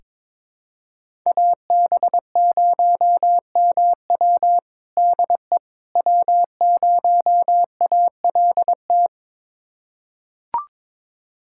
Morse code, heard as a keyed tone, about 22 wpm, 700 Hz. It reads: AB0MW DE W0ALT
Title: USA Callsign DE Callsign - 5 Characters - 22wpm